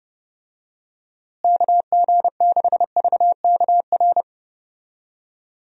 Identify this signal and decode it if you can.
Morse code, heard as a keyed tone, about 32 wpm, 700 Hz. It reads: KG6VKR
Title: USA Call Signs - 6 Characters - Repeated 3x - 30wpm